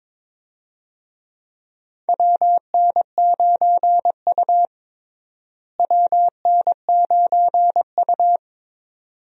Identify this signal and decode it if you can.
Morse code, heard as a keyed tone, about 22 wpm, 700 Hz. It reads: WN9U WN9U
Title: USA Call Signs - 4 Characters - Repeated 2x - 22wpm